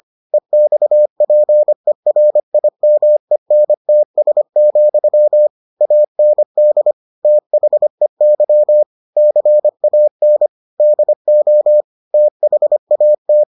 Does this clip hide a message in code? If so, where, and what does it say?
Yes: Morse code from 0.3 to 13.4 s, heard as a keyed tone, about 25 wpm, 600 Hz. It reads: EXPERIMENTS, AND THEY CAN DO THAT